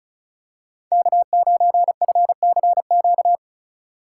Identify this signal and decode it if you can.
Morse code, heard as a keyed tone, about 36 wpm, 700 Hz. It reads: K9FCQ
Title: USA Call Signs - 5 Characters - 8x Character Spacing - 35wpm